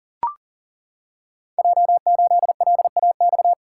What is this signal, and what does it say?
Morse code, heard as a keyed tone, about 42 wpm, 690 Hz. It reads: J8LAX
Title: Code Groups - 5 Letter:Numbers - 40wpm